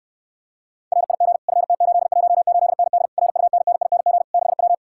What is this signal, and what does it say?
Morse code, heard as a keyed tone, about 69 wpm, 700 Hz. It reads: RIG FT818ND RUNNING 6W